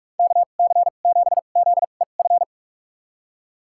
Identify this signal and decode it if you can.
Morse code, heard as a keyed tone, about 43 wpm, spaced as for 32 wpm, 700 Hz. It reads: KC7ZEF
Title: USA Call Signs - 6 Characters - 2x Character Spacing - 45wpm